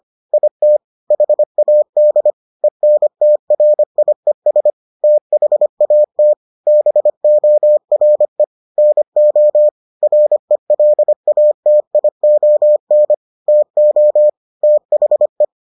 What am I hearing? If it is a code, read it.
Morse code, heard as a keyed tone, about 25 wpm, 600 Hz. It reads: IT HAD ENTRIES THAT BORE NO RELATION TO THE